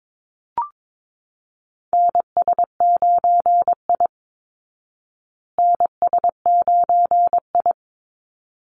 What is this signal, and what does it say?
Morse code, heard as a keyed tone, about 22 wpm, 700 Hz. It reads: NS9I NS9I
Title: Call Signs Repeated 3x - 22wpm